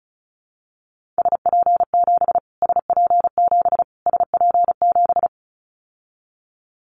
Morse code, heard as a keyed tone, about 36 wpm, 700 Hz. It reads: SP7 SP7 SP7
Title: Callsign Prefixes for Top 25 DXCC - Repeated 3x - 35wpm